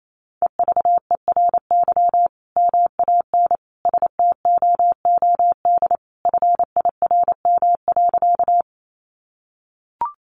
Morse code, heard as a keyed tone, about 28 wpm, 700 Hz. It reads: EVERY MAN STOOD FIRM.
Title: Sentences from Top 1400 Words 28wpm